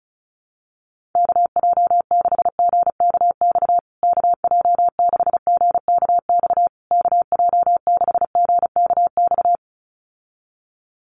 Morse code, heard as a keyed tone, about 34 wpm, 700 Hz. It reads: KJ6GKX KJ6GKX KJ6GKX
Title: USA Call Signs Repeated 3x 35wpm